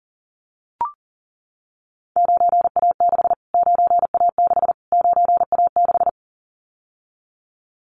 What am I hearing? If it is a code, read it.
Morse code, heard as a keyed tone, about 40 wpm, 700 Hz. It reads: 9A6 9A6 9A6
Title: Callsign Prefixes for Top 25 DXCC - Repeated 3x - 40wpm